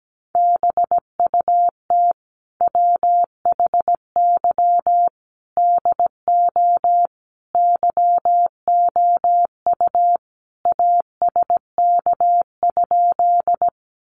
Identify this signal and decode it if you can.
Morse code, heard as a keyed tone, about 17 wpm, 700 Hz. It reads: BUT WHY DO YOU ASK?
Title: Sentences from Top 200 Words 17wpm